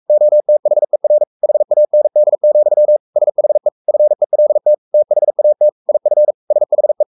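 Morse code, heard as a keyed tone, about 44 wpm, 600 Hz. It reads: OTHER HAND, SHE FELT THAT IF SHE
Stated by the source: Morse code at 43 wpm sending Ken Follett's novel Never in English, from CW med Gnister.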